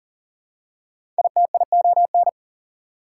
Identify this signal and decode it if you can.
Morse code, heard as a keyed tone, about 39 wpm, 700 Hz. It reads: ITION
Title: Prefixes and Suffixes 40wpm